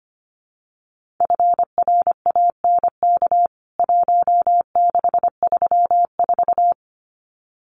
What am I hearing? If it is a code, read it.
Morse code, heard as a keyed tone, about 25 wpm, 700 Hz. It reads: FRANK 1634
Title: CWT Exchange 25wpm